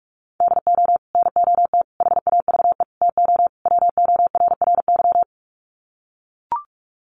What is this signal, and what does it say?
Morse code, heard as a keyed tone, about 44 wpm, 700 Hz. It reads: DO NOT HAVE TO WORRY
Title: Sets of 5 Words 45wpm